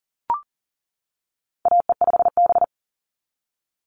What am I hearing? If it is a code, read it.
Morse code, heard as a keyed tone, about 40 wpm, 700 Hz. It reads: AE5B
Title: NA SOTA Chaser Callsigns 40wpm